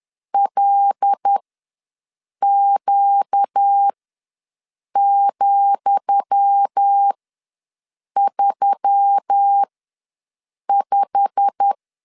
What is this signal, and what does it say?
Morse code, heard as a keyed tone, about 10 wpm, spaced as for 6 wpm, 800 Hz. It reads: LQ,35
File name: LEZIONI / 10 SIGNS / 10D COMA ws 10wpm 4314